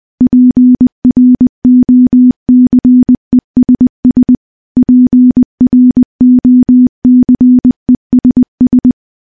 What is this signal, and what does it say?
Morse code, heard as a keyed tone, about 20 wpm, 250 Hz. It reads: PROCESS PROCESS
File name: morse-training-20WPM